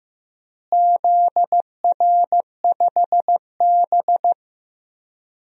Morse code, heard as a keyed tone, about 15 wpm, 700 Hz. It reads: ZR5B
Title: DXCC-100 Call Signs - 4 Characters - Repeated 3x - 15wpm